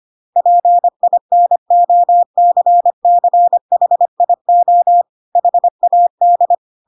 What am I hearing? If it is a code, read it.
Morse code, heard as a keyed tone, about 25 wpm, 700 Hz. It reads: PINOCCHIO HAD
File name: Pinocchio / Pinocchio ch7 25 25 ik1hge